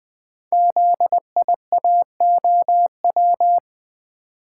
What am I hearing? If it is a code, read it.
Morse code, heard as a keyed tone, about 20 wpm, 700 Hz. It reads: ZIAOW
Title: Code Groups - 5 Letter:Numbers - 20wpm